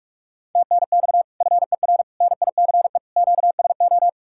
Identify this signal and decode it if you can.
Morse code, heard as a keyed tone, about 63 wpm, spaced as for 54 wpm, 700 Hz. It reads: TNX FER NICE QSO
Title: Speed-Racing Common QSO Phrases - 45wpm